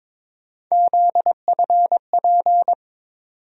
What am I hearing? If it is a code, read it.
Morse code, heard as a keyed tone, about 21 wpm, 700 Hz. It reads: ZFP
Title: USA Call Sign Postfixes - 3 Letters 22wpm